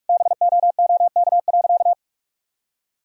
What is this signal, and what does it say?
Morse code, heard as a keyed tone, about 41 wpm, 700 Hz. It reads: BOOK.
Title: Sentences from Top 1000 Words - Review Part 1 - 45wpm